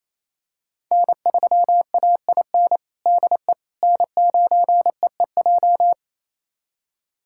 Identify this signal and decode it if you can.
Morse code, heard as a keyed tone, about 28 wpm, 700 Hz. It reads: N3AIN DE N9EEJ
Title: USA Callsign DE Callsign 28wpm